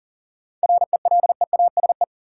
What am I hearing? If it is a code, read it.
Morse code, heard as a keyed tone, about 40 wpm, 700 Hz. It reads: RELEASE
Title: Sentences from Top 1100 Words - New Word Drill Alt - 40wpm